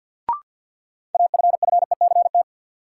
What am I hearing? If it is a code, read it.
Morse code, heard as a keyed tone, about 46 wpm, 700 Hz. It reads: AFFECT
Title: Sentences from Top 900 Words - New Word Drill - 50wpm